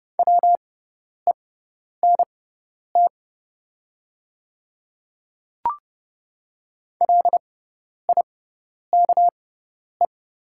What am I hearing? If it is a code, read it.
Morse code, heard as a keyed tone, about 30 wpm, spaced as for 10 wpm, 700 Hz. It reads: WENT LIKE
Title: 6x Character Spacing - Top 200 Words - 30wpm